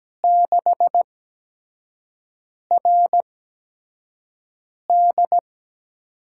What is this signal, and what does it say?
Morse code, heard as a keyed tone, about 17 wpm, spaced as for 5 wpm, 700 Hz. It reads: 6RD
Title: USA Call Signs - 8x Character Spacing - 17wpm